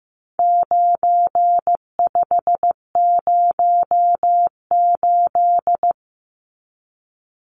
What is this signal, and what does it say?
Morse code, heard as a keyed tone, about 15 wpm, 700 Hz. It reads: 9508
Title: Numbers - 4-Digit 15wpm